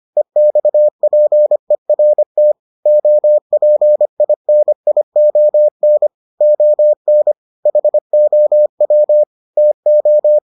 Morse code, heard as a keyed tone, about 25 wpm, 600 Hz. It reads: EXPERT OPINION ON HOW TO